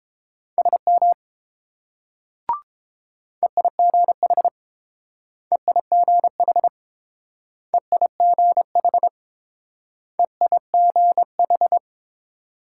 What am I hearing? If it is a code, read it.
Morse code, heard as a keyed tone, about 31 wpm, 700 Hz. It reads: SM EIGH EIGH EIGH EIGH